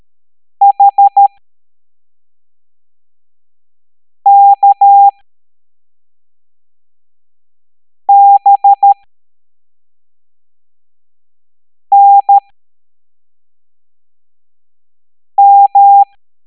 Morse code, heard as a keyed tone, about 13 wpm, spaced as for 3 wpm, 800 Hz. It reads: HKBNM
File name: hamdata / L07-EX03-HQP